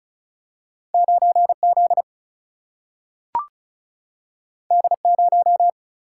Morse code, heard as a keyed tone, about 36 wpm, 700 Hz. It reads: 9Z D0
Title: Single-Letter Single-Number 35wpm